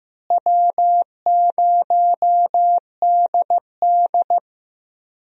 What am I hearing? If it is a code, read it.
Morse code, heard as a keyed tone, about 15 wpm, 700 Hz. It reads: W0DD
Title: USA Call Signs - 4 Characters - Repeated 2x - 15wpm